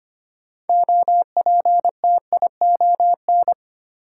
Morse code, heard as a keyed tone, about 25 wpm, 700 Hz. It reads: OPTION
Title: Sentences from Top 1300 Words - New Word Drill - 25wpm